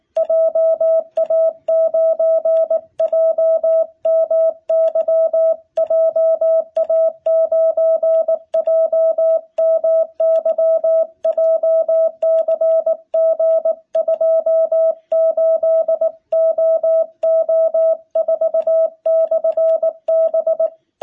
Morse code, heard as a keyed tone, about 19 wpm, 660 Hz. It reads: JA9JMY JA9JMY JCG28OO4/B
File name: jh8cbh / 2021cw